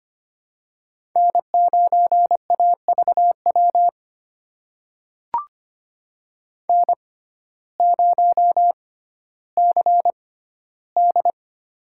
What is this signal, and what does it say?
Morse code, heard as a keyed tone, about 25 wpm, spaced as for 14 wpm, 700 Hz. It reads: N9AVW N0CD
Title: USA Call Signs - 6x Character Spacing - 25wpm